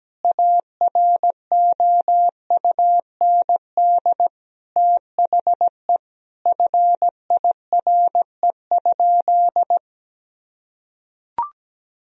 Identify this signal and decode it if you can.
Morse code, heard as a keyed tone, about 17 wpm, 700 Hz. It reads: AROUND THE FIRE?
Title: Sentences from Top 1000 Words - Review Part 3 - 17wpm